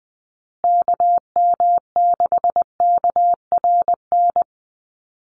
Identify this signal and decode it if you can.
Morse code, heard as a keyed tone, about 20 wpm, 700 Hz. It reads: KM6KRN
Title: USA Call Signs Repeated 2x 20wpm